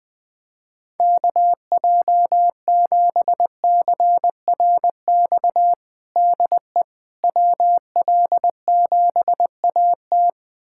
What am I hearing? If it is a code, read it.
Morse code, heard as a keyed tone, about 20 wpm, 700 Hz. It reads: KJ7CRX DE WL7AT
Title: USA Callsign DE Callsign 20wpm